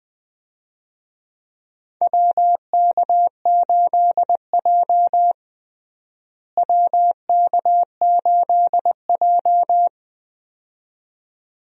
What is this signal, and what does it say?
Morse code, heard as a keyed tone, about 20 wpm, 700 Hz. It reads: WK8J WK8J
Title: USA Call Signs - 4 Characters - Repeated 2x - 20wpm